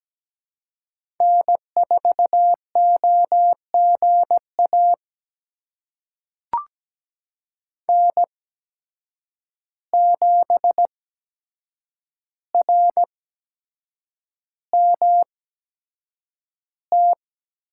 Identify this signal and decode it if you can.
Morse code, heard as a keyed tone, about 17 wpm, spaced as for 7 wpm, 700 Hz. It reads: N4OGA N7RMT
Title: USA Call Signs - 5 Characters - 8x Character Spacing - 17wpm